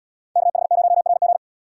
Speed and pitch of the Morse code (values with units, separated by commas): 77 wpm, 700 Hz